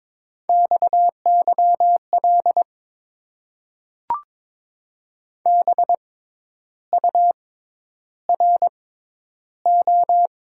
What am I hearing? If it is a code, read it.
Morse code, heard as a keyed tone, about 22 wpm, spaced as for 10 wpm, 700 Hz. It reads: XYL BURO
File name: Top 90 QSO Elements - 6x Character Spacing - 22wpm